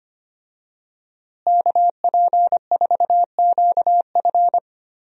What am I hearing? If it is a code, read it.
Morse code, heard as a keyed tone, about 25 wpm, 700 Hz. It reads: KP4QF
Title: DXCC-100 Call Signs - 5 Characters - Repeated 2x - 25wpm